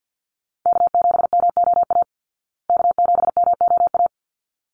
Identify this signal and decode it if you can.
Morse code, heard as a keyed tone, about 48 wpm, 700 Hz. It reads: K7MOA K7MOA
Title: Call Signs Repeated 3x - 50wpm